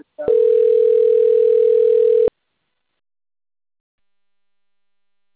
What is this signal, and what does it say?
Ringback tone (North America), 440 + 480 Hz.
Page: XV de novembro